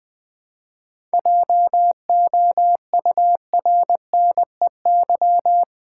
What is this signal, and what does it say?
Morse code, heard as a keyed tone, about 20 wpm, 690 Hz. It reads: JOURNEY
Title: 7 Letter Words - First 3 Letters Given - 20wpm